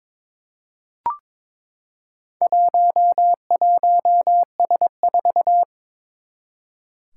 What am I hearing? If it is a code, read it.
Morse code, heard as a keyed tone, about 22 wpm, 700 Hz. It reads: 11S4
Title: Four Challenging Letter-Number Combinations 22wpm